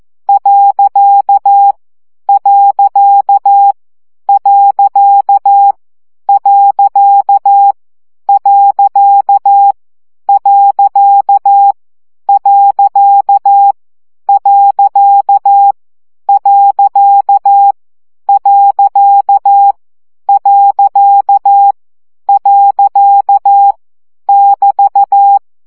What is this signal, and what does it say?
Morse code, heard as a keyed tone, about 14 wpm, 800 Hz. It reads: . . . . . . <BT>
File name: CW gyakorlas pont 12 baud